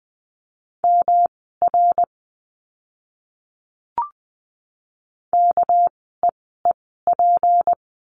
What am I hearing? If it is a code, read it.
Morse code, heard as a keyed tone, about 20 wpm, spaced as for 15 wpm, 700 Hz. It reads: MR KEEP
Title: 244 Lesson 39f - Letter z - Words 20wpm